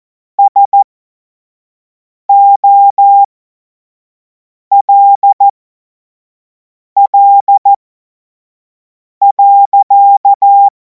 Morse code, heard as a keyed tone, about 14 wpm, spaced as for 5 wpm, 800 Hz. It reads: SOLL.
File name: Morsekurs / Witze / Witz 24. Februar 2026 25 F70